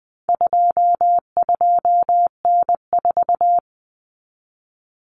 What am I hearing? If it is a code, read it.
Morse code, heard as a keyed tone, about 20 wpm, 700 Hz. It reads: 22N4
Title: Common Cut Numbers - 4-Digit 20wpm